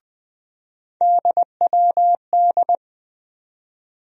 Morse code, heard as a keyed tone, about 20 wpm, 700 Hz. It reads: DWD
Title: USA Call Sign Postfixes - 3 Letters 20wpm